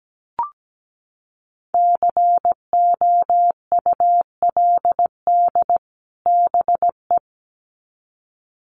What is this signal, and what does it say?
Morse code, heard as a keyed tone, about 17 wpm, 700 Hz. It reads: COULD BE
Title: Sets of 2 Words 17wpm